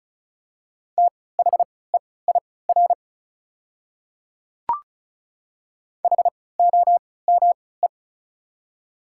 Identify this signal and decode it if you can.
Morse code, heard as a keyed tone, about 34 wpm, spaced as for 20 wpm, 700 Hz. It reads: THEIR HOME
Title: 3x Character Spacing - Top 300 Words - 35wpm